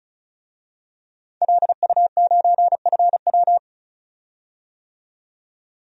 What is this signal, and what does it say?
Morse code, heard as a keyed tone, about 36 wpm, 700 Hz. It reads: LU9FW
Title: DXCC-100 Call Signs - 5 Characters - 35wpm